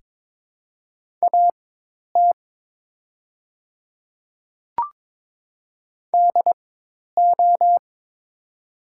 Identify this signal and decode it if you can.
Morse code, heard as a keyed tone, about 22 wpm, spaced as for 10 wpm, 700 Hz. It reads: AT DO
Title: 4x Character Spacing - Top 100 Words - 22wpm